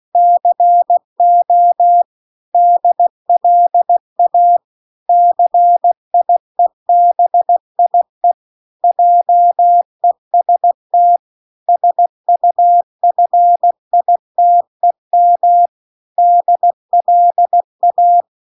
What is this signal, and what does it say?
Morse code, heard as a keyed tone, about 16 wpm, 700 Hz. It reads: CO DLA CIEBIE JEST SUFITEM DLA